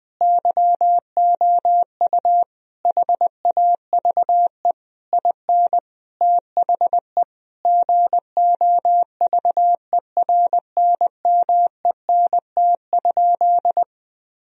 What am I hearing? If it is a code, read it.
Morse code, heard as a keyed tone, about 20 wpm, 700 Hz. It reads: YOU HAVE IN THE GOVERNMENT?
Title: Sentences from Top 1000 Words - Review Part 4 - 20wpm